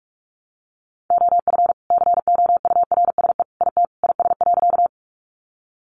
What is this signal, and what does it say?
Morse code, heard as a keyed tone, about 46 wpm, 700 Hz. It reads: OF COURSE IT IS.
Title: Sentences from Top 500 Words 45wpm